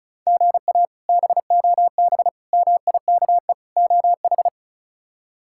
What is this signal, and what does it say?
Morse code, heard as a keyed tone, about 34 wpm, 700 Hz. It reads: GA BOB MIKE OH
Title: K1USN SST QSO Practice - 35wpm